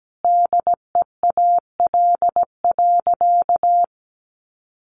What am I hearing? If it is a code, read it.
Morse code, heard as a keyed tone, about 17 wpm, 700 Hz. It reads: DEAL.
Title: Sentences from Top 1200 Words 17wpm